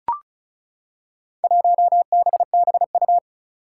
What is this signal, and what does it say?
Morse code, heard as a keyed tone, about 36 wpm, 700 Hz. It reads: 1BBU
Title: Four Challenging Letter-Number Combinations 35wpm